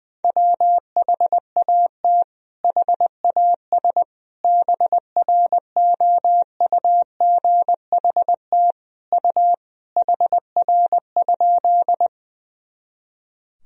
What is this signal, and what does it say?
Morse code, heard as a keyed tone, about 20 wpm, 700 Hz. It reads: WHAT HAS BROUGHT U HR?
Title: Sentences from Top 500 Words and Abbreviations 20wpm